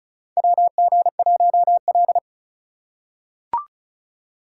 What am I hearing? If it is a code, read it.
Morse code, heard as a keyed tone, about 36 wpm, 700 Hz. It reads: WG1L
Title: USA Call Signs - 2x Character Spacing - 35wpm